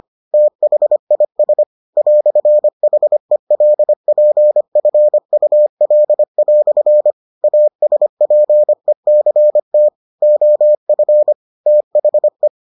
Morse code, heard as a keyed tone, about 25 wpm, 600 Hz. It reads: THIS "HELPFUL" ASPECT OF THE